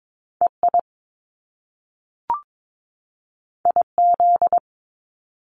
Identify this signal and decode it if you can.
Morse code, heard as a keyed tone, about 22 wpm, 700 Hz. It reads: EI IZ
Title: Top Two Letter Combinations in Words 22wpm